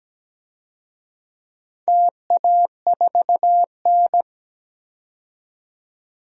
Morse code, heard as a keyed tone, about 17 wpm, 700 Hz. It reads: TA4N
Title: DXCC-100 Call Signs - 4 Characters - 17wpm